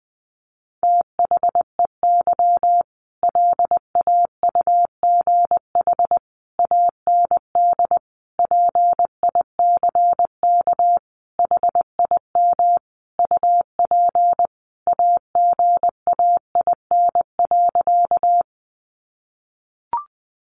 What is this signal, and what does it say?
Morse code, heard as a keyed tone, about 19 wpm, 700 Hz. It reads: THEY LAUGH AND PICK HIM UP AGAIN.
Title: Sentences from Top 1000 Words 20wpm